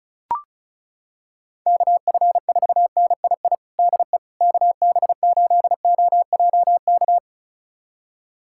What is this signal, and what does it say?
Morse code, heard as a keyed tone, about 34 wpm, 700 Hz. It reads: KF4NII DE KB8OJK
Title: USA Callsign DE Callsign - 6 Characters - 35wpm